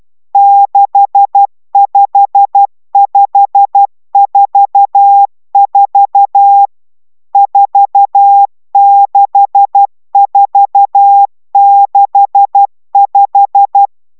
Morse code, heard as a keyed tone, about 12 wpm, 800 Hz. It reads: 65544 46465